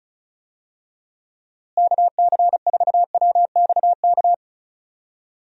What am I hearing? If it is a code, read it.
Morse code, heard as a keyed tone, about 34 wpm, 700 Hz. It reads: KC4WXK
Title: USA Call Signs - 6 Characters - Repeated 3x - 35wpm